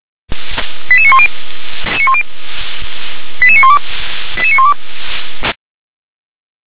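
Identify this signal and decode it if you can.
Five-tone selective call (ZVEI), heard as quick runs of five tones, tones of about 80 ms. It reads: 88910 88912 88912
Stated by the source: ZVEI five-tone call received by SDR.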